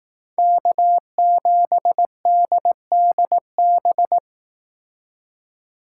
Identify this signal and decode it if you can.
Morse code, heard as a keyed tone, about 19 wpm, 700 Hz. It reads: K7DDB
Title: Speed Racing - USA 1x3 Call Signs 15wpm